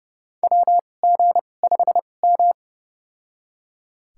Morse code, heard as a keyed tone, about 29 wpm, spaced as for 21 wpm, 700 Hz. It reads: WG5M
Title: USA Call Signs - 2x Character Spacing - 30wpm